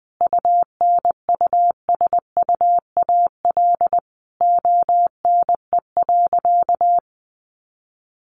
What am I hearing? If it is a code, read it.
Morse code, heard as a keyed tone, about 20 wpm, 700 Hz. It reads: UNUSUAL ONE.
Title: Sentences from Top 1400 Words 20wpm